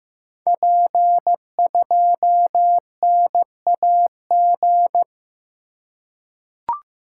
Morse code, heard as a keyed tone, about 15 wpm, 700 Hz. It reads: P2NAG
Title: Code Groups - 5 Letter:Numbers - 15wpm